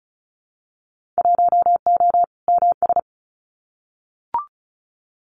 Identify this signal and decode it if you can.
Morse code, heard as a keyed tone, about 34 wpm, 690 Hz. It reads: 1O MS
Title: Winter Field Day Exchanges 35wpm